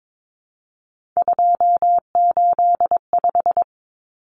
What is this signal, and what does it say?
Morse code, heard as a keyed tone, about 22 wpm, 700 Hz. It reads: 285
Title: Common Cut Numbers - 3-Digit 22wpm